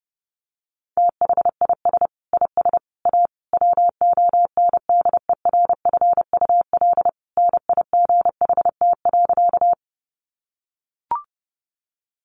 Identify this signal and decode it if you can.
Morse code, heard as a keyed tone, about 30 wpm, 700 Hz. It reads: THIS IS A WONDERFUL NIGHT.
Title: Sentences from Top 1100 Words 30wpm